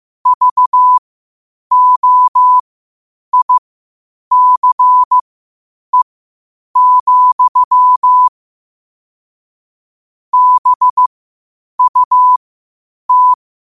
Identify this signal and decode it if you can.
Morse code, heard as a keyed tone, about 15 wpm, spaced as for 8 wpm, 1000 Hz. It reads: VOICE, BUT
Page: New quotes every day in morse code at 8 Words per minute.